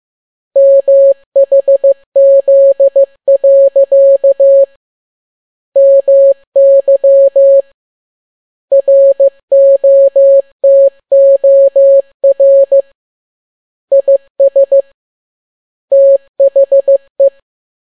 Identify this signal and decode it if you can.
Morse code, heard as a keyed tone, about 15 wpm, 550 Hz. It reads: MHZ. MY ROTOR IS THE